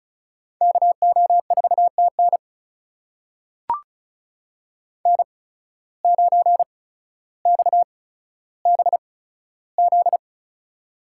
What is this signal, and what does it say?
Morse code, heard as a keyed tone, about 36 wpm, spaced as for 15 wpm, 700 Hz. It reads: KO4TN N9XBZ
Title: USA Call Signs - 5 Characters - 8x Character Spacing - 35wpm